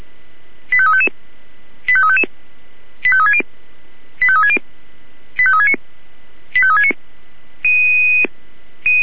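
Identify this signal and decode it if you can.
Five-tone selective call (ZVEI), heard as quick runs of five tones, tones of about 70 ms. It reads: 85355 85360 85360 85369 85369
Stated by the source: Received on SDR; ZVEI five-tone call.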